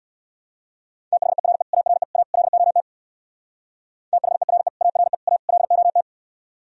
Morse code, heard as a keyed tone, about 63 wpm, 700 Hz. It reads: THERE ARE A LOT THERE ARE A LOT